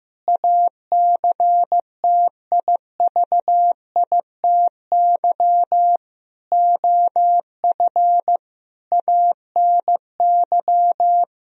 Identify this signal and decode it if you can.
Morse code, heard as a keyed tone, about 15 wpm, 700 Hz. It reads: ACTIVITY OF ANY
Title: Sentences from Top 1000 Words - Review Part 4 - 15wpm